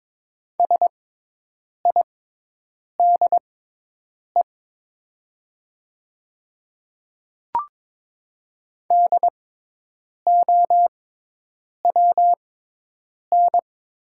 Morse code, heard as a keyed tone, about 22 wpm, spaced as for 8 wpm, 700 Hz. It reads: SIDE DOWN
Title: 6x Character Spacing - Top 100 Words - 22wpm